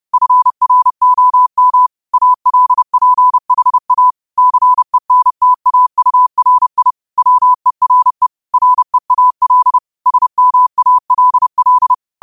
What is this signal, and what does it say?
Morse code, heard as a keyed tone, about 30 wpm, 1000 Hz. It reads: FROM ALPHA CENTAURI WERE REAL SMALL
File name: qotdfiles / 30wpm